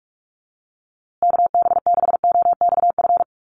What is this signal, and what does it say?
Morse code, heard as a keyed tone, about 44 wpm, 700 Hz. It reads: KB6OXF
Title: USA Call Signs Repeated 2x 45wpm